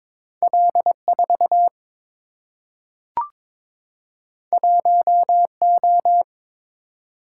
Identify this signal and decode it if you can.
Morse code, heard as a keyed tone, about 22 wpm, 700 Hz. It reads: L 4 1 O
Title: Single-Letter Single-Number 22wpm